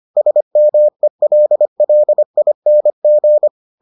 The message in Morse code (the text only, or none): SMELLING